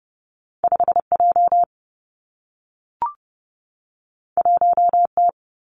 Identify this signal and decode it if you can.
Morse code, heard as a keyed tone, about 30 wpm, 700 Hz. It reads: 5 J 1 T
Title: Two Challenging Letter-Number Combinations 30wpm